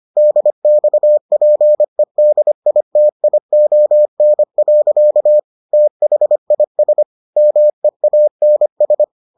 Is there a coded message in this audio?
Yes: Morse code from 0.2 to 9.0 s, heard as a keyed tone, about 25 wpm, 600 Hz. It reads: DXPEDITION. THIS MEANS